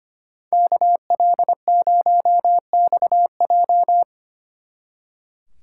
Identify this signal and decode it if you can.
Morse code, heard as a keyed tone, about 32 wpm, 700 Hz. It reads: KL0XJ